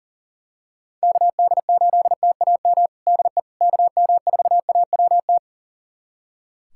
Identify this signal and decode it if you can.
Morse code, heard as a keyed tone, about 37 wpm, 700 Hz. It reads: KD8TAM DE KM4AWT
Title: USA Callsign DE Callsign - 6 Characters - 40wpm